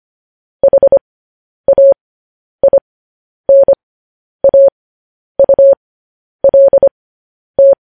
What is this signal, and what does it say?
Morse code, heard as a keyed tone, about 25 wpm, spaced as for 10 wpm, 550 Hz. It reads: HAINAULT
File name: podcast / News headlines-1x-25x10